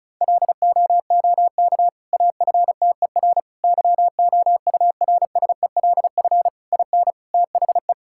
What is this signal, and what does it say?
Morse code, heard as a keyed tone, about 36 wpm, 700 Hz. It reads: LOOK AFTER YOURSELF IN THE
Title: Sentences from Top 1000 Words - Review Part 6 - 35wpm